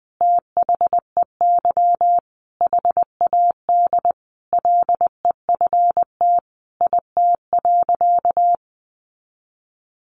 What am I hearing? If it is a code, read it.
Morse code, heard as a keyed tone, about 20 wpm, 700 Hz. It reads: THEY HAD LEFT IT.
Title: A Sound Of Thunder 20wpm